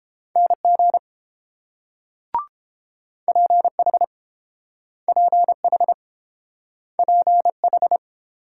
Morse code, heard as a keyed tone, about 30 wpm, 700 Hz. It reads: NG PH PH PH